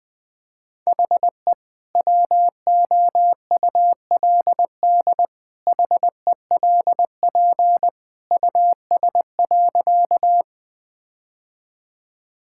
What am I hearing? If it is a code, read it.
Morse code, heard as a keyed tone, about 20 wpm, 700 Hz. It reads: HE WOULD HELP US.
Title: Sentences with 2-Word Phrases - Spoken Beginning - 20wpm